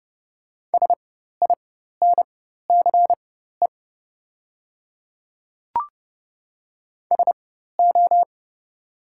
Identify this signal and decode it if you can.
Morse code, heard as a keyed tone, about 30 wpm, spaced as for 14 wpm, 700 Hz. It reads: SINCE SO
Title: 4x Character Spacing - Top 300 Words - 30wpm